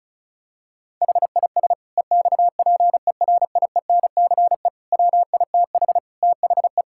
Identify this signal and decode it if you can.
Morse code, heard as a keyed tone, about 34 wpm, 700 Hz. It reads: HIS EXPERIENCE WITH THE
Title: Sentences from Top 1000 Words - Review Part 6 - 35wpm